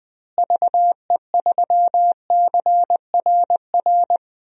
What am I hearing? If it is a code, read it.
Morse code, heard as a keyed tone, about 20 wpm, 700 Hz. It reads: VE3CRR
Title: DXCC-100 Call Signs - 6 Characters - Repeated 3x - 20wpm